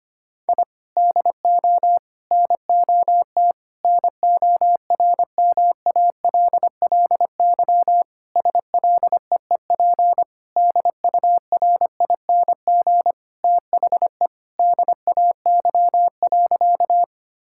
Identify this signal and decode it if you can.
Morse code, heard as a keyed tone, about 25 wpm, 700 Hz. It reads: I DO NOT NORMALLY SLEEP DURING THE DAY.
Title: Sentences from Top 1000 Words - Review Part 6 - 25wpm